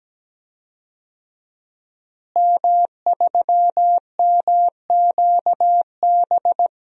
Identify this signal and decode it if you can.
Morse code, heard as a keyed tone, about 17 wpm, 700 Hz. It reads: M3MQB
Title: DXCC-100 Call Signs - 5 Characters - Repeated 3x - 17wpm